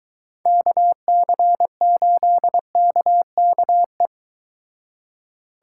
Morse code, heard as a keyed tone, about 22 wpm, 700 Hz. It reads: KC8KKE